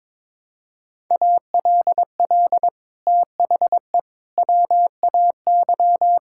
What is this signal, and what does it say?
Morse code, heard as a keyed tone, about 22 wpm, 700 Hz. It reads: ALL THE WAY
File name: practice / 22wpm / Sets of 6 Words - 22wpm